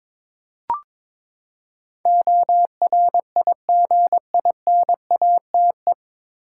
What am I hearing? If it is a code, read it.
Morse code, heard as a keyed tone, about 22 wpm, 700 Hz. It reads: ORIGINATE
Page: Morse Code Ninja practice 22wpm